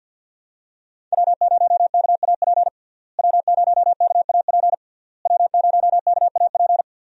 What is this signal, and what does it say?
Morse code, heard as a keyed tone, about 53 wpm, 700 Hz. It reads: W0KAP W0KAP W0KAP
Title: USA Call Signs - 5 Characters - Repeated 3x - 50wpm